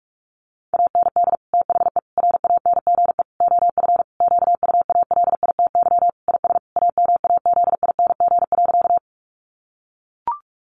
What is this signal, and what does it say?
Morse code, heard as a keyed tone, about 44 wpm, 700 Hz. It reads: AND THE RANGE OF QUALITY IS AMAZING.
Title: Sentences from Top 1500 Words 45wpm